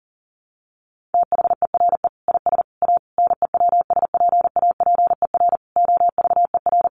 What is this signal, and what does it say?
Morse code, heard as a keyed tone, about 37 wpm, 700 Hz. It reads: THERE IS A NEWSPAPER OVER
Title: Sentences from Top 1100 Words 40wpm